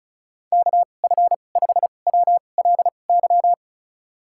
Morse code, heard as a keyed tone, about 34 wpm, spaced as for 25 wpm, 700 Hz. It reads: KF5WLY
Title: USA Call Signs - 6 Characters - 2x Character Spacing - 35wpm